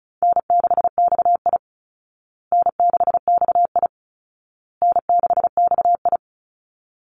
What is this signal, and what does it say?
Morse code, heard as a keyed tone, about 34 wpm, 700 Hz. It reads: N6XI N6XI N6XI
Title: Call Signs Repeated 2x - 35wpm